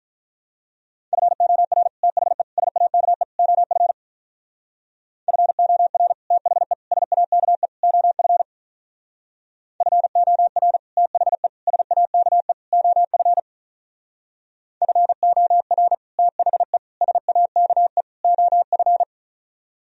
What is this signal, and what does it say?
Morse code, heard as a keyed tone, about 45 wpm, 700 Hz. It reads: FOR THE SAKE OF FOR THE SAKE OF FOR THE SAKE OF FOR THE SAKE OF